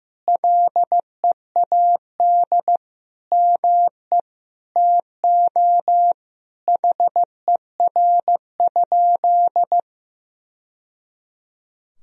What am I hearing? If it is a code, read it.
Morse code, heard as a keyed tone, about 16 wpm, 700 Hz. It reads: LEAD ME TO HER?
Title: Sentences from Top 1000 Words - Review Part 2 - 15wpm